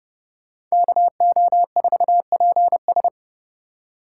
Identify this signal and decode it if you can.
Morse code, heard as a keyed tone, about 37 wpm, 700 Hz. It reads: KO4PS
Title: Speed Racing - USA 2x2 Call Signs 30wpm